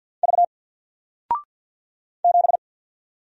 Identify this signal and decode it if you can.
Morse code, heard as a keyed tone, about 50 wpm, 700 Hz. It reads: V 7
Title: Single Character (Letter:Number:Punctuation:Prosign) 50wpm